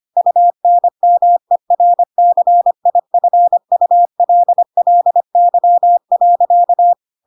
Morse code, heard as a keyed tone, about 25 wpm, 700 Hz. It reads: UNMERCIFULLY.
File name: Pinocchio / Pinocchio ch1 25 25 ik1hge